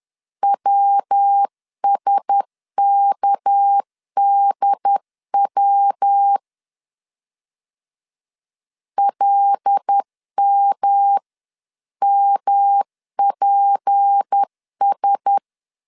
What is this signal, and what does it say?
Morse code, heard as a keyed tone, about 10 wpm, 800 Hz. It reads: WSKDW LMMPS
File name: CW / DET / 1 6 1500C S C 10wpm 4252